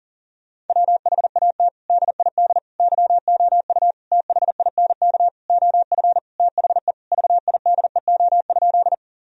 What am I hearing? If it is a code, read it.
Morse code, heard as a keyed tone, about 40 wpm, 700 Hz. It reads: WHAT DID YOU THINK OF THE VIDEO?
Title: Sentences from Top 1000 Words - Review Part 4 - 40wpm